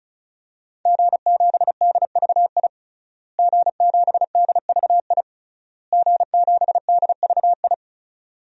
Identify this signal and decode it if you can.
Morse code, heard as a keyed tone, about 36 wpm, 690 Hz. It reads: G7DVI G7DVI G7DVI
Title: DXCC-100 Call Signs - 5 Characters - Repeated 3x - 35wpm